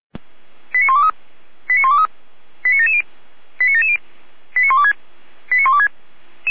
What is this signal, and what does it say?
Five-tone selective call (ZVEI), heard as quick runs of five tones, tones of about 70 ms. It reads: 89123 89123 89770 89770 89126 89126